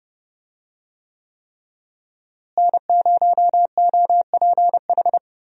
Morse code, heard as a keyed tone, about 32 wpm, 700 Hz. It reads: N0OPH
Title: USA Call Signs - 5 Characters - Repeated 3x - 30wpm